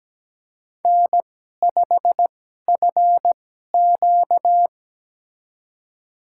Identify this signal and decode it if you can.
Morse code, heard as a keyed tone, about 17 wpm, spaced as for 12 wpm, 700 Hz. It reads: N5FQ
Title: USA Call Signs - 4 Characters - 2x Character Spacing - 17wpm